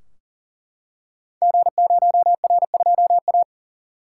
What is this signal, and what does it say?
Morse code, heard as a keyed tone, about 39 wpm, 700 Hz. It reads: G0R2A
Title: Code Groups - 5 Letter:Numbers - 40wpm